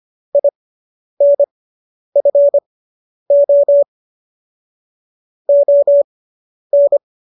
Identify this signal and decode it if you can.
Morse code, heard as a keyed tone, about 25 wpm, spaced as for 10 wpm, 570 Hz. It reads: INFO ON